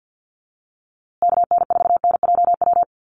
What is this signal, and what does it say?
Morse code, heard as a keyed tone, about 48 wpm, 700 Hz. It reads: KN4NJW
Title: USA Call Signs Repeated 2x 50wpm